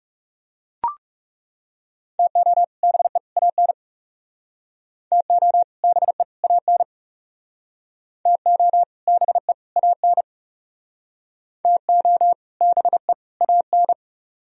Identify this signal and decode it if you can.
Morse code, heard as a keyed tone, about 36 wpm, 700 Hz. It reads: TO BE AN TO BE AN TO BE AN TO BE AN
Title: Speed-Racing - Sets of 3 Words 30wpm